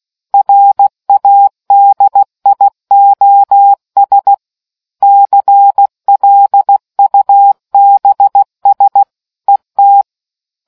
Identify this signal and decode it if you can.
Morse code, heard as a keyed tone, about 16 wpm, 800 Hz. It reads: RADIOS CLUBS ET